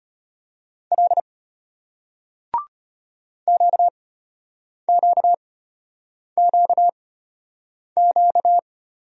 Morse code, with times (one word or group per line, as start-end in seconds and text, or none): 0.9-1.2 L
3.5-8.6 QQQQ